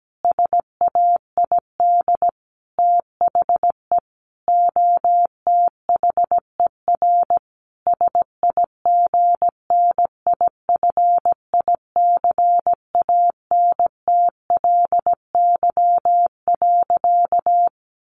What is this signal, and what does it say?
Morse code, heard as a keyed tone, about 17 wpm, 700 Hz. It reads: SAID THE OTHER SIGNIFICANTLY.
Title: Sentences from Top 1500 Words 17wpm